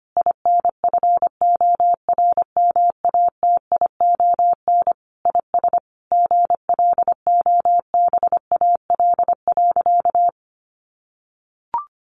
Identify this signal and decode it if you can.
Morse code, heard as a keyed tone, about 25 wpm, 700 Hz. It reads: INFORMATION IS GLOBAL.
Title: Sentences from Top 1200 Words 25wpm